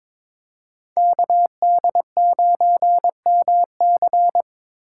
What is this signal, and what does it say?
Morse code, heard as a keyed tone, about 22 wpm, 700 Hz. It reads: KD9MC
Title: USA Call Signs - 5 Characters - 2x Character Spacing - 22wpm